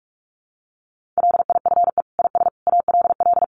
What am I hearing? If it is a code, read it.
Morse code, heard as a keyed tone, about 44 wpm, 700 Hz. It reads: LIFE IS ALL
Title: Sentences from Top 300 Words 45wpm